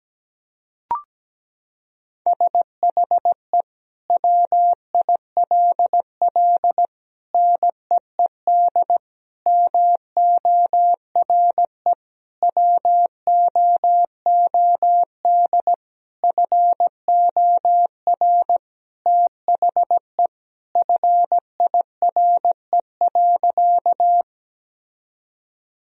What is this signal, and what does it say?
Morse code, heard as a keyed tone, about 17 wpm, 700 Hz. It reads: SHE WILL NEED MORE WOOD FOR THE FIRE.
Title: Sentences from Top 1000 Words - Review Part 2 - 17wpm